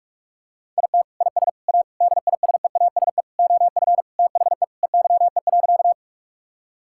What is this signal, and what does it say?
Morse code, heard as a keyed tone, about 41 wpm, 700 Hz. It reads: IT IS A DISEASE OF THE EYE.
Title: Sentences from Top 1000 Words - Review Part 6 - 45wpm